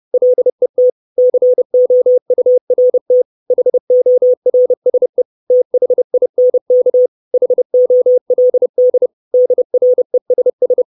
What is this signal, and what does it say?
Morse code, heard as a keyed tone, about 30 wpm, 500 Hz. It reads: LET COURT HORSE THINK HOLD DRESS
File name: 632 Words 30WPM